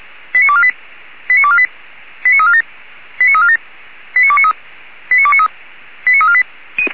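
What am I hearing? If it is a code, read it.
Five-tone selective call (ZVEI), heard as quick runs of five tones, tones of about 70 ms. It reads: 89248 89248 89347 89347 89293 89293 89348
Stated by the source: ZVEI five-tone selective call 89406 received on SDR.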